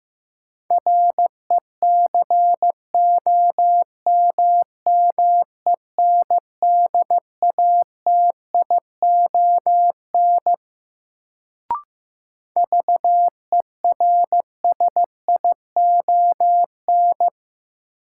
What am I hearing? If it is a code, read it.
Morse code, heard as a keyed tone, about 15 wpm, 700 Hz. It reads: RECOMMENDATION VERSION
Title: Sentences from Top 1400 Words - New Word Drill Alt - 15wpm